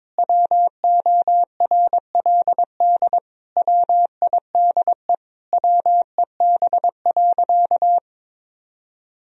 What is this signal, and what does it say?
Morse code, heard as a keyed tone, about 22 wpm, 700 Hz. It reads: WORLD WIDE WEB.
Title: Sentences from Top 1000 Words - Review Part 4 - 22wpm